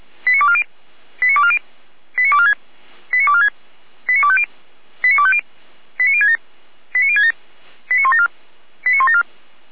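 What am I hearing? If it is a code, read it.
Five-tone selective call (ZVEI), heard as quick runs of five tones, tones of about 70 ms. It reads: 89240 89240 89246 89246 89250 89250 89067 89067 89174 89174
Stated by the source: ZVEI five-tone call received by SDR, archived by Feuerwehr Orlamünde.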